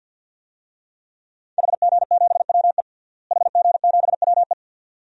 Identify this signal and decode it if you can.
Morse code, heard as a keyed tone, about 50 wpm, 700 Hz. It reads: HG7WE HG7WE
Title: DXCC-100 Call Signs - 5 Characters - Repeated 2x - 50wpm